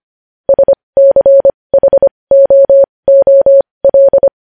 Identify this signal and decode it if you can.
Morse code, heard as a keyed tone, about 25 wpm, spaced as for 20 wpm, 550 Hz. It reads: SCHOOL